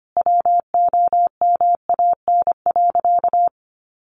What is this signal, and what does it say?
Morse code, heard as a keyed tone, about 25 wpm, 700 Hz. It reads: WOMAN.
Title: Sentences from Top 1500 Words 25wpm